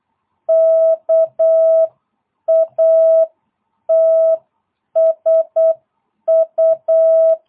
Morse code, heard as a keyed tone, about 8 wpm, 650 Hz. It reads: KATSU